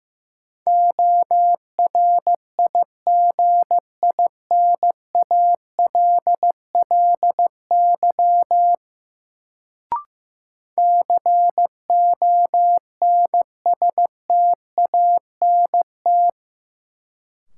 Morse code, heard as a keyed tone, about 15 wpm, 700 Hz. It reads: ORIGINALLY CONSTANT
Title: Sentences from Top 1500 Words - New Word Drill Alt - 15wpm